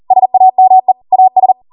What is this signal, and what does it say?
Morse code, heard as a keyed tone, about 40 wpm, 750 Hz. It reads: SAME AS